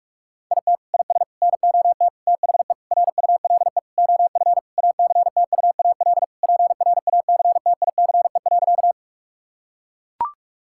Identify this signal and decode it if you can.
Morse code, heard as a keyed tone, about 42 wpm, 700 Hz. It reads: IT IS NOT THE RULE OF ACTUAL PRACTICE.
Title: Sentences from Top 1000 Words - Review Part 5 - 45wpm